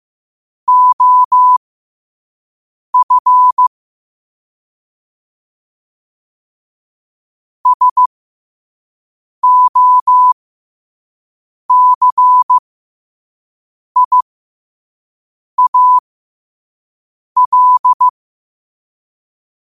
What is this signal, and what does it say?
Morse code, heard as a keyed tone, about 15 wpm, spaced as for 5 wpm, 1000 Hz. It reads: OF SOCIAL
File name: qotdfiles / 5wpm